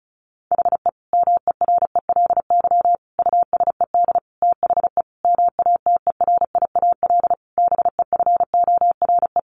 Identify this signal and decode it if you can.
Morse code, heard as a keyed tone, about 34 wpm, 700 Hz. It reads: HE MERELY USED THE MATERIAL BEFORE
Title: Sentences from Top 1400 Words 35wpm